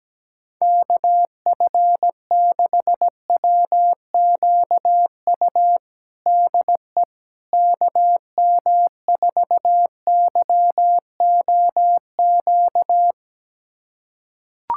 Morse code, heard as a keyed tone, about 17 wpm, 700 Hz. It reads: KF6WQU DE KM4YOQ
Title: USA Callsign DE Callsign - 6 Characters - 17wpm